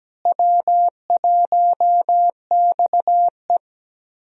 Morse code, heard as a keyed tone, about 17 wpm, 700 Hz. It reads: W1XE
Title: USA Call Signs - 4 Characters - Repeated 3x - 17wpm